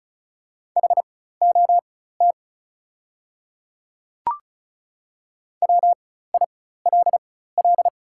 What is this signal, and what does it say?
Morse code, heard as a keyed tone, about 36 wpm, spaced as for 16 wpm, 700 Hz. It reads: HOT WILL
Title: 4x Character Spacing - Top 100 Words - 35wpm